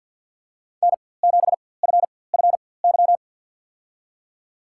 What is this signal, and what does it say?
Morse code, heard as a keyed tone, about 50 wpm, spaced as for 23 wpm, 700 Hz. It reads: N7FFY
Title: USA Call Signs - 5 Characters - 4x Character Spacing - 50wpm